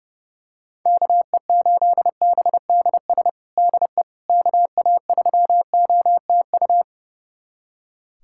Morse code, heard as a keyed tone, about 30 wpm, 700 Hz. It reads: KE8BDS DE KA3OTU
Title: USA Callsign DE Callsign 30wpm